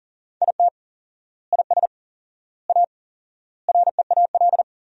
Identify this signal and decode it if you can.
Morse code, heard as a keyed tone, about 40 wpm, 700 Hz. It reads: IT IS A REAL
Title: 4x Word-Spacing - Sentences from Top 500 Words - 40wpm